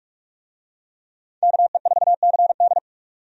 Morse code, heard as a keyed tone, about 46 wpm, 700 Hz. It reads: KE4CD
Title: USA Call Signs - 5 Characters - 45wpm